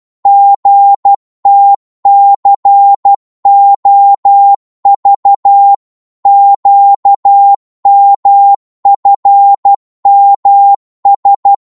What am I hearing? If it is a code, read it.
Morse code, heard as a keyed tone, about 12 wpm, 800 Hz. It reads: GTCOV QMFMS
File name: Morse / ltr-12wpm